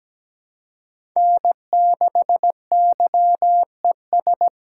Morse code, heard as a keyed tone, about 17 wpm, 700 Hz. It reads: N6YES
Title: USA Call Signs - 8x Character Spacing - 17wpm